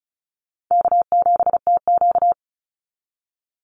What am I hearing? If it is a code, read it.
Morse code, heard as a keyed tone, about 34 wpm, 700 Hz. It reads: K7TQ
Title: NA SOTA Chaser Callsigns 35wpm